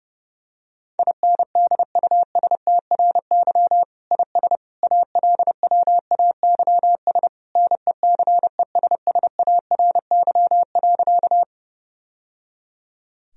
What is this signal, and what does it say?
Morse code, heard as a keyed tone, about 30 wpm, 700 Hz. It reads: INDUSTRY IS ALWAYS NECESSARY.
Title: Sentences from Top 1000 Words - Review Part 4 - 30wpm